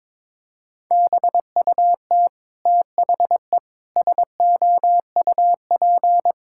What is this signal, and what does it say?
Morse code, heard as a keyed tone, about 22 wpm, 700 Hz. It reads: BUT THE SOUP
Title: Sentences from Top 1000 Words - Review Part 6 - 22wpm